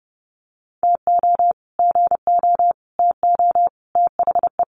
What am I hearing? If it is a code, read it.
Morse code, heard as a keyed tone, about 28 wpm, 700 Hz. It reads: TO GO TO THE
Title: Sets of 5 Words Encore 30wpm